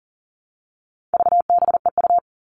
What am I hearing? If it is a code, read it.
Morse code, heard as a keyed tone, about 40 wpm, 700 Hz. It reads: VBEU
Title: All Cut Numbers - 4-Digit 40wpm